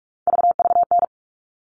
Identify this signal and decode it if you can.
Morse code, heard as a keyed tone, about 44 wpm, 700 Hz. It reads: VVN
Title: All Cut Numbers - 3-Digit 45wpm